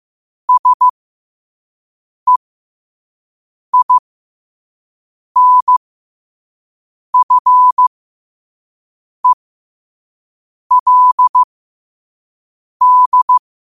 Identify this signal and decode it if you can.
Morse code, heard as a keyed tone, about 15 wpm, spaced as for 5 wpm, 1000 Hz. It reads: SEINFELD
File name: qotdfiles / 5wpm